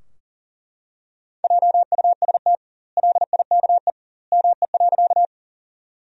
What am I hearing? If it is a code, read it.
Morse code, heard as a keyed tone, about 40 wpm, 700 Hz. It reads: JUST LIKE ME.
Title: 2x Word-Spacing - Sentences from Top 200 Words - 40wpm